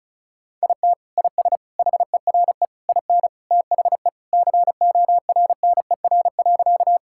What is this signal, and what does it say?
Morse code, heard as a keyed tone, about 36 wpm, 700 Hz. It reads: IT IS HERE IN THE CORNER.
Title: Sentences from Top 1600 Words 35wpm